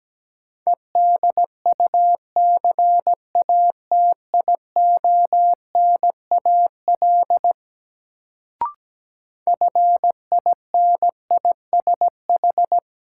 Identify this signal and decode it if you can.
Morse code, heard as a keyed tone, about 17 wpm, 700 Hz. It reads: EDUCATIONAL FINISH
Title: Sentences from Top 1200 Words - New Word Drill Alt - 17wpm